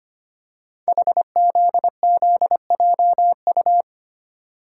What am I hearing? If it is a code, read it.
Morse code, heard as a keyed tone, about 25 wpm, 700 Hz. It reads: HZZJU
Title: Code Groups - 5 Letters - 25wpm